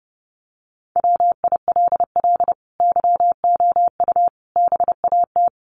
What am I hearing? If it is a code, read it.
Morse code, heard as a keyed tone, about 30 wpm, 700 Hz. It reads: WILL YOU BAT
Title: Sentences from Top 1500 Words 30wpm